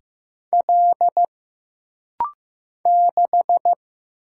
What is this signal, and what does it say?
Morse code, heard as a keyed tone, about 15 wpm, 700 Hz. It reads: L 6
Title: Single Letter-Number Alt - Rapid-Fire 15wpm